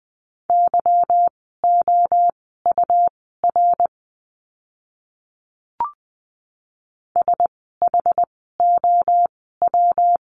Morse code, heard as a keyed tone, about 20 wpm, spaced as for 15 wpm, 700 Hz. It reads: YOUR SHOW
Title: 206 Lesson 33f - Letter j - Words 20wpm